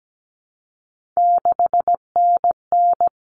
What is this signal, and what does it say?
Morse code, heard as a keyed tone, about 17 wpm, 700 Hz. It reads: 6NN
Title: Common Cut Numbers - 3-Digit 17wpm